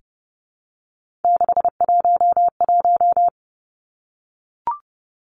Morse code, heard as a keyed tone, about 30 wpm, 700 Hz. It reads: 611
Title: Three Challenging Letter-Number Combinations 30wpm